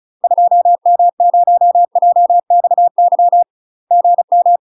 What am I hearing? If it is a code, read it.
Morse code, heard as a keyed tone, about 34 wpm, 700 Hz. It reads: 2M0JXY GM
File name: UK MASTER QSOs TRAINING / 35WPM / UK MASTER QSOs 0021